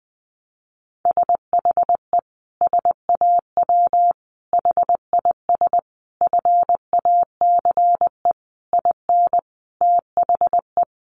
Morse code, heard as a keyed tone, about 20 wpm, 700 Hz. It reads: SHE SAW HIS FACE IN THE
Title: Sentences from Top 1300 Words 20wpm